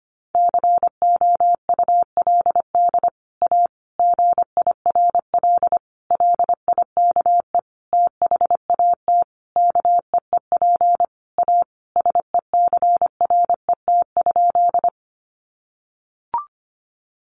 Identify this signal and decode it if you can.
Morse code, heard as a keyed tone, about 25 wpm, 700 Hz. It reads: COULD A GIRL LIKE THAT KEEP A SECRET?
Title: Sentences from Top 1200 Words 25wpm